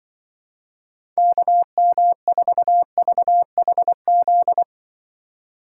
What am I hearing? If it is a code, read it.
Morse code, heard as a keyed tone, about 25 wpm, 700 Hz. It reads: KM4VHZ
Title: Speed Racing - USA 2x3 Call Signs 20wpm